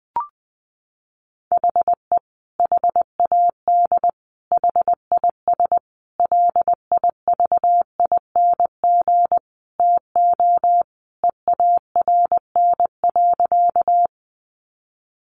Morse code, heard as a keyed tone, about 20 wpm, 700 Hz. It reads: HE HAD HIS LIVING TO EARN.
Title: Sentences from Top 1500 Words 20wpm